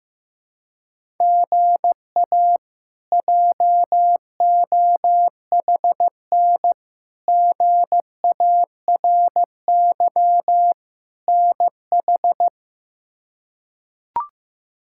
Morse code, heard as a keyed tone, about 15 wpm, 700 Hz. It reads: GA JOHN GARY NH
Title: K1USN SST QSO Practice - 15wpm